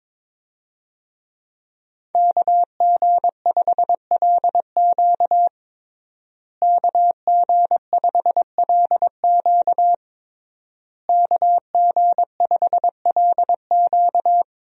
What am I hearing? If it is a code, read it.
Morse code, heard as a keyed tone, about 22 wpm, 700 Hz. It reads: KG5LQ KG5LQ KG5LQ
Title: USA Call Signs - 5 Characters - Repeated 3x - 22wpm